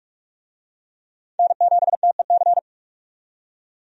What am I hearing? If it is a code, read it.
Morse code, heard as a keyed tone, about 47 wpm, 700 Hz. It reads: N7TEC
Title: USA Call Signs - 5 Characters - 45wpm